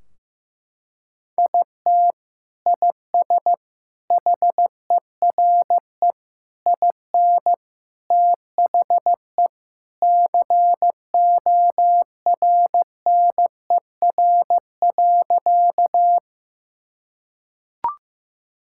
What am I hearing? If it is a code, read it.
Morse code, heard as a keyed tone, about 16 wpm, 700 Hz. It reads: IT IS HERE IN THE CORNER.
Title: Sentences from Top 1600 Words 15wpm